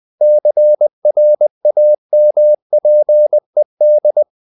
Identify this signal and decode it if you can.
Morse code, heard as a keyed tone, about 20 wpm, 600 Hz. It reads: CRAMPED
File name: the old man and the sea / the old man and the sea part 1 20wpm